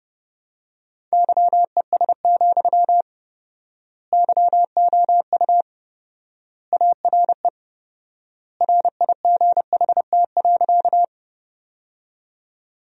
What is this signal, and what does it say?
Morse code, heard as a keyed tone, about 30 wpm, 700 Hz. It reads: YES, YOU ARE RIGHT.
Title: 4x Word-Spacing - Sentences from Top 500 Words - 30wpm